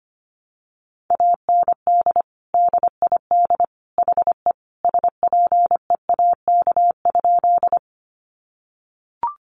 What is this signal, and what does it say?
Morse code, heard as a keyed tone, about 25 wpm, 700 Hz. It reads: AND DID HE SPEAK?
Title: Sentences from Top 1100 Words 25wpm